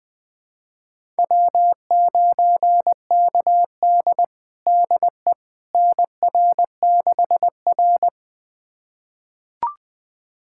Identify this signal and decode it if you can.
Morse code, heard as a keyed tone, about 20 wpm, 700 Hz. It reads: W9KD DE NR6R
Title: USA Callsign DE Callsign - 4 Characters - 20wpm